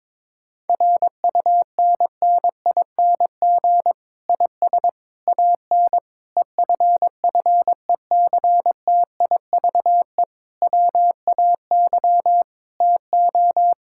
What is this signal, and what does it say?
Morse code, heard as a keyed tone, about 22 wpm, 700 Hz. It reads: RUNNING IS AN EFFECTIVE WAY TO
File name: Difficult Sentences with Hints - 22wpm